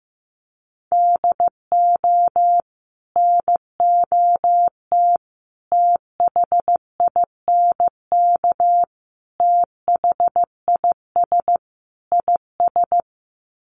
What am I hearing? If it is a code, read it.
Morse code, heard as a keyed tone, about 15 wpm, 700 Hz. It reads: DO NOT THINK THIS IS
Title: Sets of 5 Words Encore2 15wpm